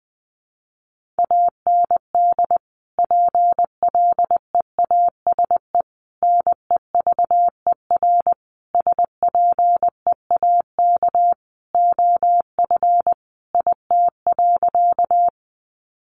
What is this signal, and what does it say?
Morse code, heard as a keyed tone, about 20 wpm, 700 Hz. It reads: AND PLEASE NEVER SPEAK OF IT.
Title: Sentences from Top 1500 Words 20wpm